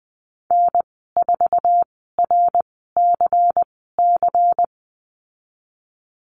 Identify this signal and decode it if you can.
Morse code, heard as a keyed tone, about 20 wpm, spaced as for 15 wpm, 700 Hz. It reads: N4RCC
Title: 167 Lesson 27g - Number 7 - Call Signs 20wpm